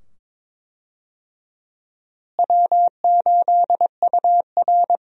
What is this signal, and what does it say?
Morse code, heard as a keyed tone, about 22 wpm, 700 Hz. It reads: W8UR
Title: USA Call Signs - 4 Characters - Repeated 2x - 22wpm